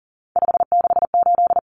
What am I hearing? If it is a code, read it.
Morse code, heard as a keyed tone, about 40 wpm, 700 Hz. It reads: 568
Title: Common Cut Numbers - 3-Digit 40wpm